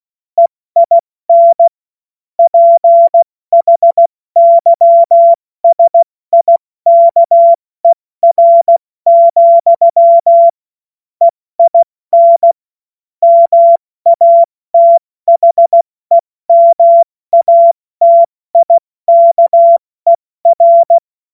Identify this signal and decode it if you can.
Morse code, heard as a keyed tone, about 16 wpm, spaced as for 14 wpm, 680 Hz. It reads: EIN PHYSIKER, EIN MATHEMATIKER